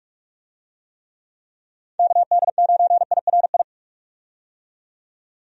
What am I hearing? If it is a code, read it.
Morse code, heard as a keyed tone, about 44 wpm, 700 Hz. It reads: KD9IRI
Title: USA Call Signs - 6 Characters - 45wpm